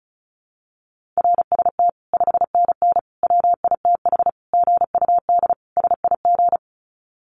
Morse code, heard as a keyed tone, about 34 wpm, 700 Hz. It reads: RST 5NN WITH GUD SIG
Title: Common QSO Phrases 35wpm